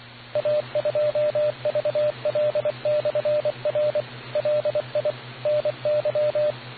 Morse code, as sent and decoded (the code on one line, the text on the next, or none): .- ..--- ...- .-.. -..-. .-. / .-.. .. / -. -.--
A2VL/R LI NY